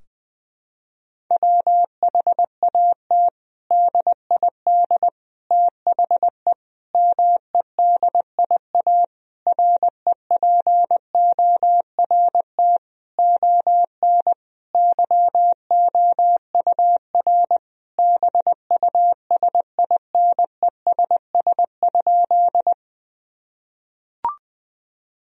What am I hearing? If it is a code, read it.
Morse code, heard as a keyed tone, about 20 wpm, 700 Hz. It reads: WHAT DID THE MEDIA REPORT ON YOUR BUSINESS?
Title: Sentences from Top 1000 Words - Review Part 4 - 20wpm